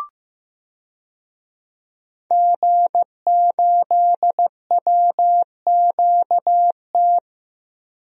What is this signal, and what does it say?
Morse code, heard as a keyed tone, about 15 wpm, 700 Hz. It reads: G8WQT
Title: DXCC-100 Call Signs - 5 Characters - Repeated 2x - 15wpm